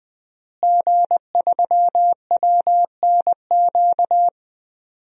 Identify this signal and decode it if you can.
Morse code, heard as a keyed tone, about 20 wpm, 700 Hz. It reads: G3WNQ
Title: DXCC-100 Call Signs - 5 Characters - Repeated 3x - 20wpm